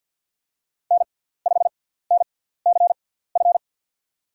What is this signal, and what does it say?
Morse code, heard as a keyed tone, about 53 wpm, spaced as for 17 wpm, 700 Hz. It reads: N5NCF
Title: USA Call Signs - 5 Characters - 6x Character Spacing - 50wpm